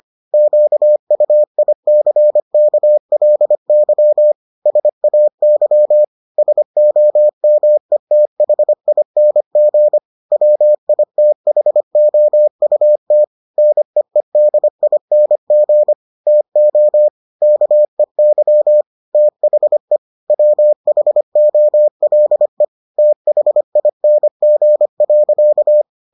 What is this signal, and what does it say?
Morse code, heard as a keyed tone, about 25 wpm, 600 Hz. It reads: QUICKLY SAY SOMETHING WITHOUT NEEDING TO KEY THE WHOLE THING.